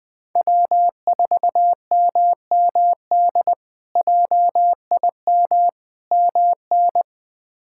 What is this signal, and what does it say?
Morse code, heard as a keyed tone, about 20 wpm, 700 Hz. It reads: W4MMD JIM MN
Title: K1USN SST QSO Practice - 20wpm